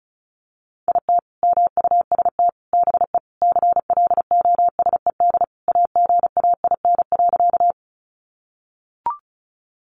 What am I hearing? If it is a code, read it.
Morse code, heard as a keyed tone, about 34 wpm, 700 Hz. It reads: IT MUST BE CLOSED AGAIN.
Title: Sentences from Top 1400 Words 35wpm